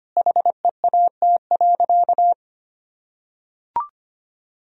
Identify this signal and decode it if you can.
Morse code, heard as a keyed tone, about 25 wpm, 700 Hz. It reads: HEAT.
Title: Sentences from Top 1000 Words - Review Part 3 - 25wpm